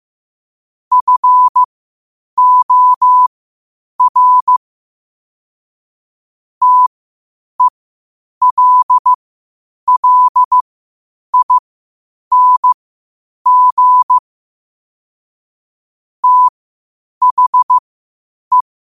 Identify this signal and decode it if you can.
Morse code, heard as a keyed tone, about 15 wpm, spaced as for 8 wpm, 1000 Hz. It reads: FOR TELLING THE
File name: qotdfiles / 8wpm